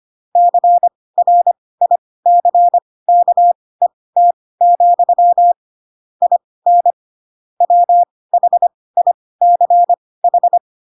Morse code, heard as a keyed tone, about 25 wpm, spaced as for 18 wpm, 700 Hz. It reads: CRICKET, IN WHICH
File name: Pinocchio / Pinocchio ch4 12 25 ik1hge